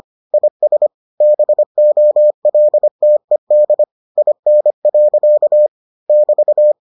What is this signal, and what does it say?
Morse code, heard as a keyed tone, about 25 wpm, 600 Hz. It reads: IS BOLTED IN. <BT>